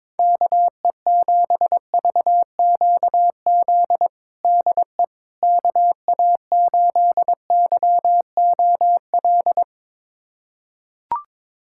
Morse code, heard as a keyed tone, about 22 wpm, 700 Hz. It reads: KE7VQZ DE KA8YOL
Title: USA Callsign DE Callsign - 6 Characters - 22wpm